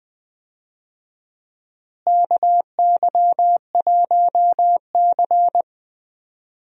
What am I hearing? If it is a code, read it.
Morse code, heard as a keyed tone, about 20 wpm, 700 Hz. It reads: KY1C
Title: USA Call Signs - 4 Characters - Repeated 2x - 20wpm